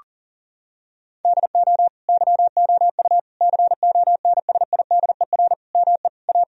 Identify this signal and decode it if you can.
Morse code, heard as a keyed tone, about 40 wpm, 700 Hz. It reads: DO YOU CONSIDER ME A
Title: Sentences from Top 1000 Words - Review Part 4 - 40wpm